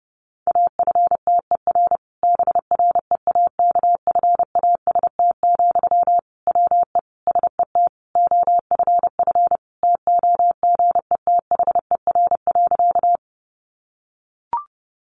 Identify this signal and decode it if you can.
Morse code, heard as a keyed tone, about 30 wpm, 700 Hz. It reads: AFTER BREAKFAST, WE SET OFF TOGETHER.
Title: Sentences from Top 1500 Words 30wpm